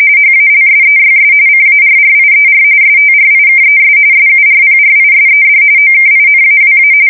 RTTY, about 45.45 baud. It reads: HALSEYS CARRIER STRIKES SURPRISED OKINAWA
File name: fsk170 / USS-New-Jersey